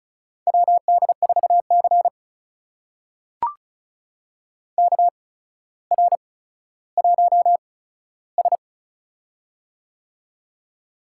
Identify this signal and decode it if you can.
Morse code, heard as a keyed tone, about 36 wpm, spaced as for 15 wpm, 700 Hz. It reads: WD4C KR1S
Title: USA Call Signs - 4 Characters - 8x Character Spacing - 35wpm